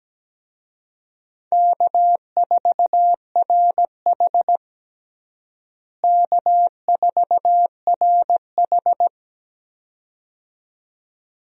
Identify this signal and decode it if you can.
Morse code, heard as a keyed tone, about 17 wpm, 700 Hz. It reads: K4RH K4RH
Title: USA Call Signs - 4 Characters - Repeated 2x - 17wpm